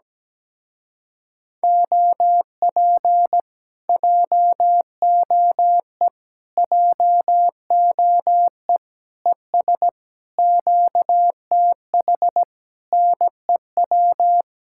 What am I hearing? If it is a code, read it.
Morse code, heard as a keyed tone, about 17 wpm, 700 Hz. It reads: OP JOE JOE ES QTH NEW
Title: Speed-Racing Common QSO Phrases Encore - 17wpm